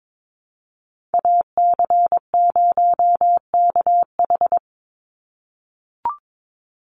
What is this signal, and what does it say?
Morse code, heard as a keyed tone, about 22 wpm, 700 Hz. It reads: AC0KH
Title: USA Call Signs 22wpm